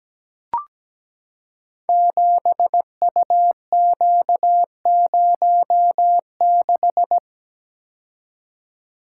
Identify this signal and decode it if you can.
Morse code, heard as a keyed tone, about 17 wpm, 700 Hz. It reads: 7UQ06
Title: Code Groups - 5 Letter:Numbers - 17wpm